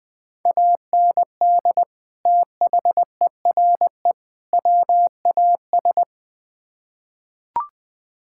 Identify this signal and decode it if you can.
Morse code, heard as a keyed tone, about 20 wpm, 700 Hz. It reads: AND THERE WAS
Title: Build A Phrase - 3-Word Phrases - 20wpm